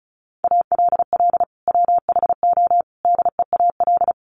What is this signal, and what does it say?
Morse code, heard as a keyed tone, about 34 wpm, 700 Hz. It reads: ALL WHO DEAL
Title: Sentences from Top 1400 Words 35wpm